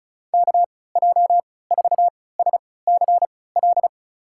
Morse code, heard as a keyed tone, about 34 wpm, spaced as for 20 wpm, 700 Hz. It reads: KJ4SCL
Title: USA Call Signs - 6 Characters - 3x Character Spacing - 35wpm